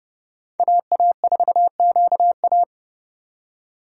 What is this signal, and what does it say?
Morse code, heard as a keyed tone, about 30 wpm, 700 Hz. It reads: AA4QA
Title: USA Call Signs - 5 Characters - 3x Character Spacing - 30wpm